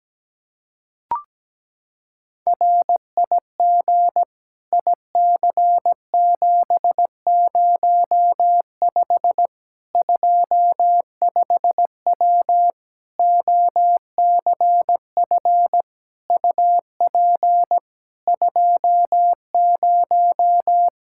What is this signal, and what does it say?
Morse code, heard as a keyed tone, about 17 wpm, 700 Hz. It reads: RIG IC705 25W OCF UP 20
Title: K1USN SST QSO Practice 17wpm